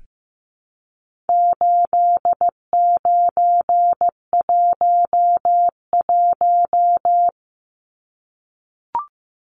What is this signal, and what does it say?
Morse code, heard as a keyed tone, about 15 wpm, 700 Hz. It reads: 8911
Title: Numbers - 4-Digit 15wpm